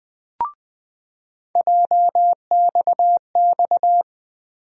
Morse code, heard as a keyed tone, about 20 wpm, 700 Hz. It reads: JXX
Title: USA Call Sign Postfixes - 3 Letters 20wpm